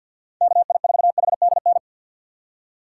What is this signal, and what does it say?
Morse code, heard as a keyed tone, about 51 wpm, 700 Hz. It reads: KI4HDN
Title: USA Call Signs - 3x Character Spacing - 50wpm